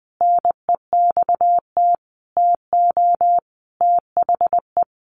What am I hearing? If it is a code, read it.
Morse code, heard as a keyed tone, about 20 wpm, 700 Hz. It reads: NEXT TO THE
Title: A Sound Of Thunder 20wpm